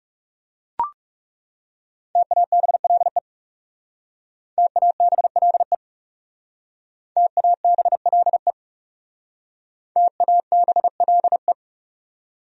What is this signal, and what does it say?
Morse code, heard as a keyed tone, about 36 wpm, 700 Hz. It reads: TABLE TABLE TABLE TABLE
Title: Speed-Racing Top 500 Words 30wpm